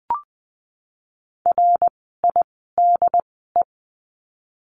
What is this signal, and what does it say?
Morse code, heard as a keyed tone, about 20 wpm, spaced as for 14 wpm, 700 Hz. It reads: RIDE
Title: 068 Lesson 12f - Letter d - Words 20wpm